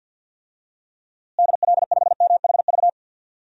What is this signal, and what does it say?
Morse code, heard as a keyed tone, about 46 wpm, 700 Hz. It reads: DL5MHV
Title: DXCC-100 Call Signs - 6 Characters - Repeated 2x - 50wpm